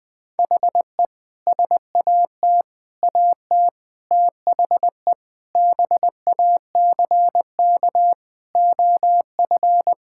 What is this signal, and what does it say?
Morse code, heard as a keyed tone, about 21 wpm, 700 Hz. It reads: HE SAT AT THE BACK OF
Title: Sentences with 3-Word Phrases - Spoken Beginning - 20wpm